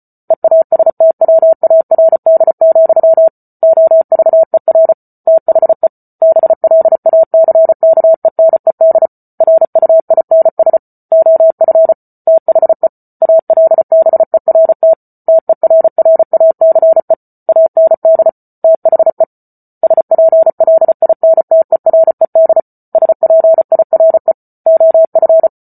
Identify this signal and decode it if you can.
Morse code, heard as a keyed tone, about 34 wpm, 650 Hz. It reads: EASTWARD, OVER THE BLACKENED RUINS OF THE ALBERT TERRACE AND THE SPLINTERED SPIRE OF